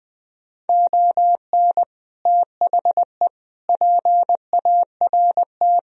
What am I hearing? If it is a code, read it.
Morse code, heard as a keyed tone, about 21 wpm, 700 Hz. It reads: ON THE PART